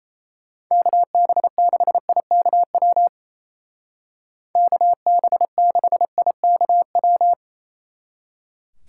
Morse code, heard as a keyed tone, about 34 wpm, 700 Hz. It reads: KB6IKW KB6IKW
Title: Speed Racing - USA 2x3 Call Signs 28wpm